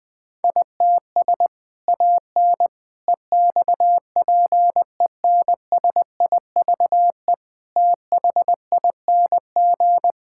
Morse code, heard as a keyed tone, about 20 wpm, 700 Hz. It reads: ITS AN EXPENSIVE THING
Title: Sentences from Top 1000 Words - Review Part 6 - 20wpm